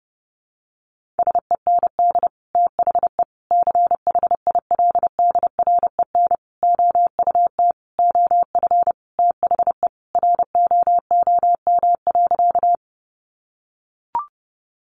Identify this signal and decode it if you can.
Morse code, heard as a keyed tone, about 30 wpm, 700 Hz. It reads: SEND THE CHILDREN OUT OF THE ROOM.
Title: Sentences from Top 1500 Words 30wpm